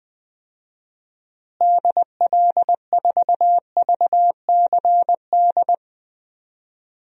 Morse code, heard as a keyed tone, about 21 wpm, 700 Hz. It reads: DL4VCD
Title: DXCC-100 Call Signs - 6 Characters - Repeated 2x - 20wpm